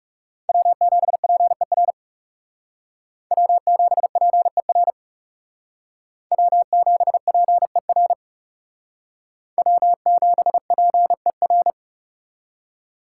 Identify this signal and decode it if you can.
Morse code, heard as a keyed tone, about 40 wpm, 700 Hz. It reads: W7PER W7PER W7PER W7PER